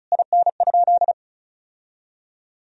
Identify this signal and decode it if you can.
Morse code, heard as a keyed tone, about 34 wpm, 700 Hz. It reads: IN?
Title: Sentences from Top 1000 Words - Review Part 2 - 35wpm